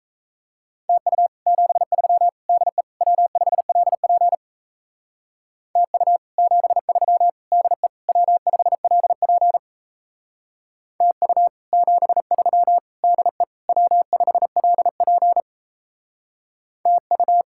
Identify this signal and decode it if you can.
Morse code, heard as a keyed tone, about 37 wpm, 700 Hz. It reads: TU 73 DE W5LP TU 73 DE W5LP TU 73 DE W5LP TU